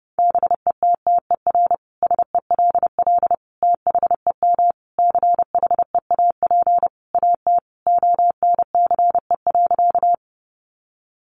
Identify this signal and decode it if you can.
Morse code, heard as a keyed tone, about 30 wpm, 700 Hz. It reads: BETTER SELL THEM CHEAP AT ONCE.
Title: Sentences from Top 1200 Words 30wpm